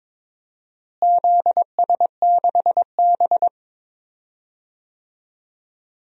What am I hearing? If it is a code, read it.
Morse code, heard as a keyed tone, about 22 wpm, 700 Hz. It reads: ZS6B
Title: DXCC-100 Call Signs - 4 Characters - Repeated 3x - 22wpm